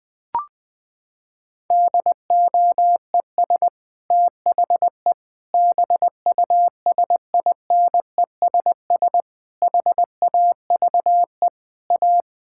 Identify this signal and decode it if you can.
Morse code, heard as a keyed tone, about 20 wpm, 700 Hz. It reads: DOES THE BUSINESS HAVE A
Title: Sentences from Top 1600 Words 20wpm